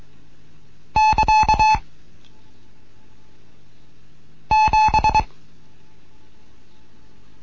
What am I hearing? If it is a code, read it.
Morse code, heard as a keyed tone, about 23 wpm, 860 Hz. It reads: <KA> 7